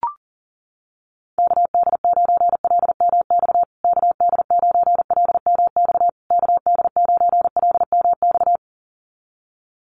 Morse code, heard as a keyed tone, about 40 wpm, 700 Hz. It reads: KD9LMX KD9LMX KD9LMX
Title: USA Call Signs Repeated 3x 40wpm